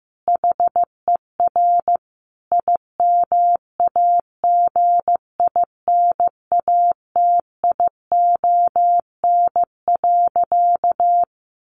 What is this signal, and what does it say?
Morse code, heard as a keyed tone, about 15 wpm, 700 Hz. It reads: HER IMAGINATION.
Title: Sentences from Top 1400 Words 15wpm